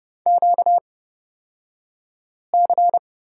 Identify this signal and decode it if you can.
Morse code, heard as a keyed tone, about 31 wpm, spaced as for 5 wpm, 700 Hz. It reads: QC
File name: Lesson 38d - Letter q - Two Characters 30wpm